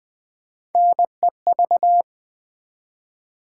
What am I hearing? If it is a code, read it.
Morse code, heard as a keyed tone, about 20 wpm, 700 Hz. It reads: NEV
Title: Build A Word - 1 Letter At A Time - 3 to 5-Letter Words - 20wpm